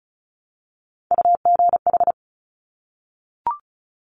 Morse code, heard as a keyed tone, about 36 wpm, 700 Hz. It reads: UGH
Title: Top Three Letter Combinations in Words 35wpm